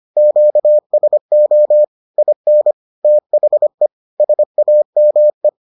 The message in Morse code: QSO IN THE SAME

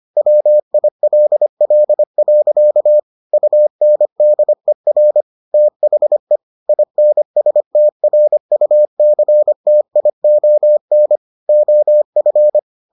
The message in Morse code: WILL. UNDER THE INSTRUCTION OF

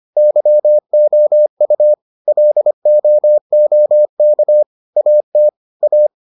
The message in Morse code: YOU LOOK AT A

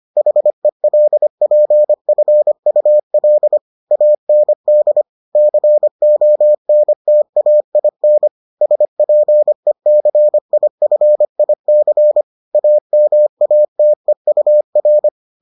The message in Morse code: HELPFUL AND CONTAIN SPECIFIC AMATEUR

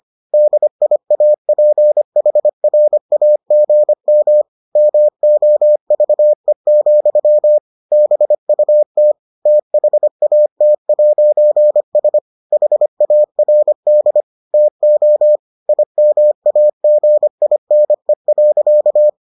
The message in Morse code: DIAPHRAGM MOVE, BUT THAT'S HARD TO IMAGINE.

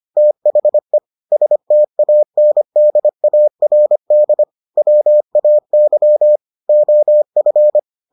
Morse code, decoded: THE STANDARD WAY OF

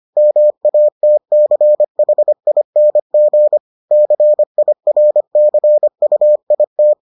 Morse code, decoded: MATCHING CIRCUIT

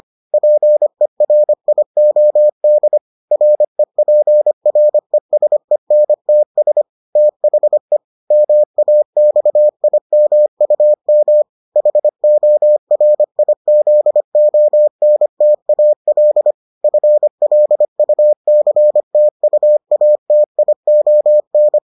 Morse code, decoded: PERIOD REPRESENTS THE MAXIMUM HORIZONTAL FLUCTUATION

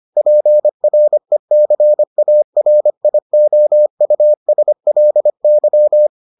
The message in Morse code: PRECARIOUSLY

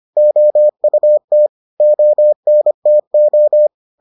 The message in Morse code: OUT ONTO

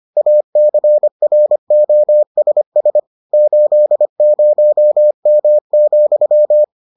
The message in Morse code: ACROSS 80M,

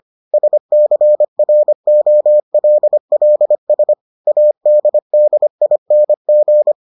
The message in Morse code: SCROLLS ADDING

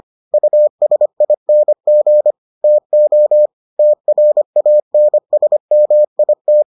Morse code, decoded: USING TO TRANSMIT

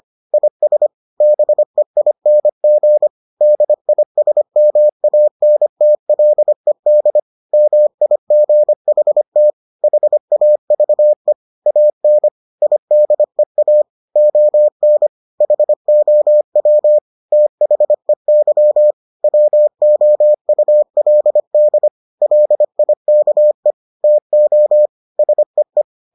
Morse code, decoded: IS BEING DISMANTLED MIGHT HAVE AN IDEA ON HOW THEY WOULD LIKE TO SEE

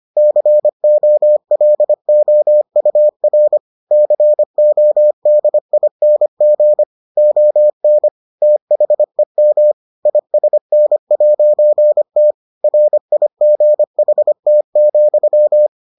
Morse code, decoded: COLOUR CODING ON THEM ISN'T RIGHT,

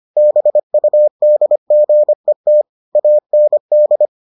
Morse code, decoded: BUDGET AND

